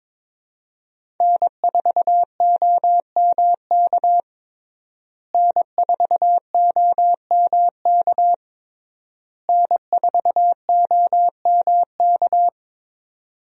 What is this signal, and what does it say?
Morse code, heard as a keyed tone, about 22 wpm, 700 Hz. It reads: N4OMK N4OMK N4OMK
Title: USA Call Signs - 5 Characters - Repeated 3x - 22wpm